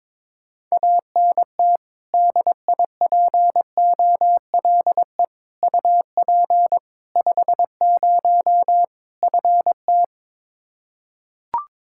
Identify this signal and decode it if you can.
Morse code, heard as a keyed tone, about 22 wpm, 700 Hz. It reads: ANT DIPOLE UP 50 FT
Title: Practice QSO - Exchange 2 - Info - 22wpm